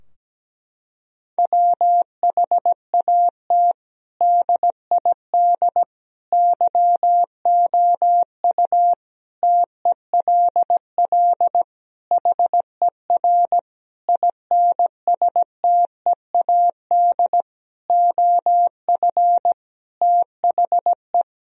Morse code, heard as a keyed tone, about 17 wpm, 700 Hz. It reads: WHAT DID YOU TELL HER INSTEAD OF THE
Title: Sentences from Top 1000 Words - Review Part 4 - 17wpm